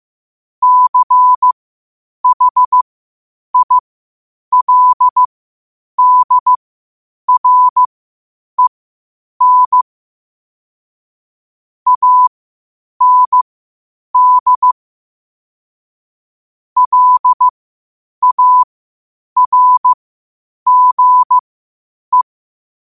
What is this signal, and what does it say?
Morse code, heard as a keyed tone, about 15 wpm, spaced as for 8 wpm, 1000 Hz. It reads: CHILDREN AND LARGE
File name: qotdfiles / 8wpm